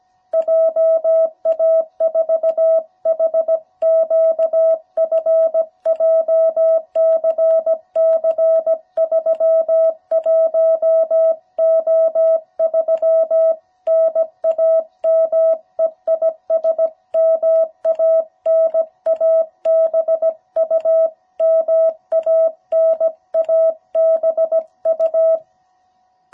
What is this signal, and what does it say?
Morse code, heard as a keyed tone, about 17 wpm, 660 Hz. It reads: JA4HQF JCC31O3 NAME IS MANABU MANABU
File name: jh8cbh / 2021cw